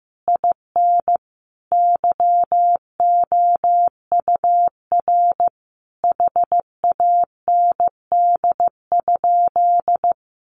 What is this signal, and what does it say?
Morse code, heard as a keyed tone, about 15 wpm, 700 Hz. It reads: IN YOUR HAND?
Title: Sentences from Top 200 Words 15wpm